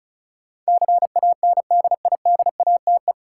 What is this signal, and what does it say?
Morse code, heard as a keyed tone, about 34 wpm, 700 Hz. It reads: CANDIDATE
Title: Sentences from Top 1600 Words - New Word Drill - 35wpm